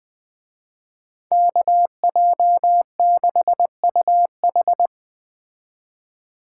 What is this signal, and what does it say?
Morse code, heard as a keyed tone, about 20 wpm, 700 Hz. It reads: KJ6UH
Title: Speed Racing - USA 2x2 Call Signs 17wpm